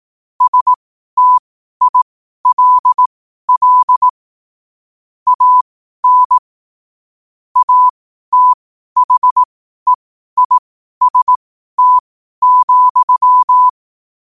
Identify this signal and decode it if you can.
Morse code, heard as a keyed tone, about 18 wpm, spaced as for 12 wpm, 1000 Hz. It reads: STILL AN ATHEIST,
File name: qotdfiles / 12wpm